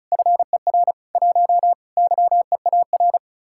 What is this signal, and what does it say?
Morse code, heard as a keyed tone, about 45 wpm, 700 Hz. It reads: FER 1 YEAR